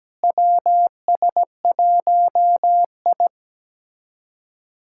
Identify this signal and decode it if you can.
Morse code, heard as a keyed tone, about 17 wpm, 700 Hz. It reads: WS1I
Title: USA Callsign DE Callsign - 4 Characters - 17wpm